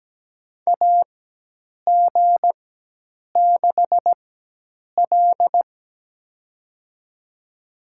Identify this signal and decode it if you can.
Morse code, heard as a keyed tone, about 17 wpm, spaced as for 8 wpm, 700 Hz. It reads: AG6L
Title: USA Call Signs - 4 Characters - 4x Character Spacing - 17wpm